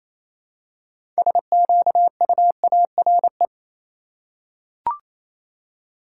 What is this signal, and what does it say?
Morse code, heard as a keyed tone, about 28 wpm, 700 Hz. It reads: SQUARE
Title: Six Letter Words - 28wpm